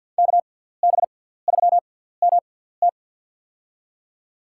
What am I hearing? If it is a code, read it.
Morse code, heard as a keyed tone, about 49 wpm, spaced as for 17 wpm, 700 Hz. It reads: KB3MT
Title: USA Call Signs - 5 Characters - 6x Character Spacing - 50wpm